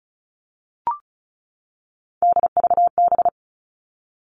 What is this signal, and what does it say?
Morse code, heard as a keyed tone, about 34 wpm, 700 Hz. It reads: DVB
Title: All Cut Numbers - 3-Digit 35wpm